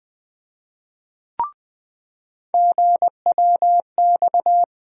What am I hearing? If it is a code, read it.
Morse code, heard as a keyed tone, about 20 wpm, 700 Hz. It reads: GWX
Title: USA Call Sign Postfixes - 3 Letters 20wpm